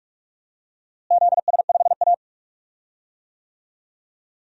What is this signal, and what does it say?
Morse code, heard as a keyed tone, about 41 wpm, 700 Hz. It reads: ZS5A
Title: DXCC-100 Call Signs - 4 Characters - Repeated 2x - 45wpm